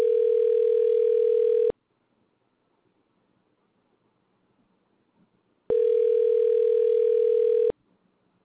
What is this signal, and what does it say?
Ringback tone (North America), 440 + 480 Hz.